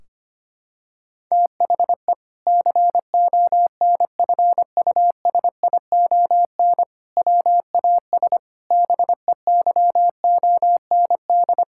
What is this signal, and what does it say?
Morse code, heard as a keyed tone, about 25 wpm, 700 Hz. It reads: THE CONFUSION WAS BEYOND
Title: Sentences from Top 1600 Words 25wpm